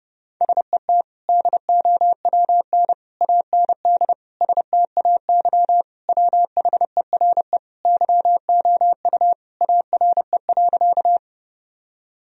Morse code, heard as a keyed tone, about 30 wpm, 700 Hz. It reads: SET DOWN AND STAY WHERE YOU ARE.
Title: Sentences from Top 1000 Words - Review Part 1 - 30wpm